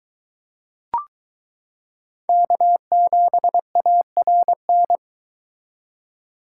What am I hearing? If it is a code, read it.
Morse code, heard as a keyed tone, about 22 wpm, 700 Hz. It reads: K7ARN